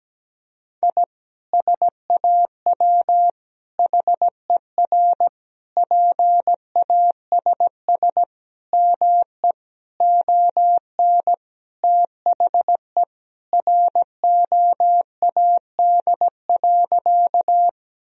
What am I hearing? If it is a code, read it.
Morse code, heard as a keyed tone, about 17 wpm, 700 Hz. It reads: I SAW HER PASS ME ON THE ROAD.
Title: Sentences from Top 1000 Words - Review Part 2 - 17wpm